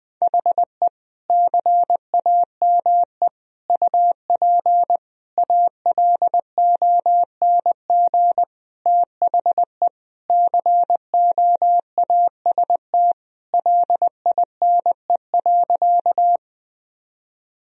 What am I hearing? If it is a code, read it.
Morse code, heard as a keyed tone, about 21 wpm, 700 Hz. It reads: HE CAME UP ALONG THE COAST LINE.
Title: Sentences from Top 1000 Words - Review Part 4 - 20wpm